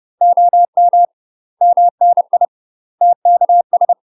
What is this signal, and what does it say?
Morse code, heard as a keyed tone, about 30 wpm, 700 Hz. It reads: OM MNI TKS
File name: UK MASTER QSOs TRAINING / 30WPM / UK MASTER QSOs 0012